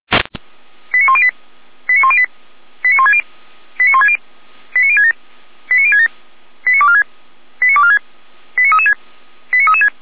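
Five-tone selective call (ZVEI), heard as quick runs of five tones, tones of about 70 ms. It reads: 89108 89108 89160 89160 89067 89067 89246 89246 89336 89336
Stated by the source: Five-tone ZVEI selective call received by SDR.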